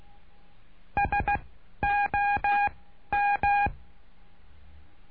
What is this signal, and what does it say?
Morse code, heard as a keyed tone, about 16 wpm, 810 Hz. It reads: S O M
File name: cursos / CURSO DE TELEGRAFIA / Telegrafia - Trilha 09